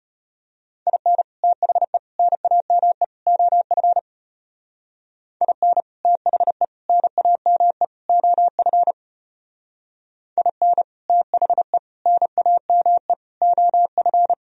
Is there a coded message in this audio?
Yes: Morse code from 0.9 to 14.3 s, heard as a keyed tone, about 34 wpm, 700 Hz. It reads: IN THE NAME OF IN THE NAME OF IN THE NAME OF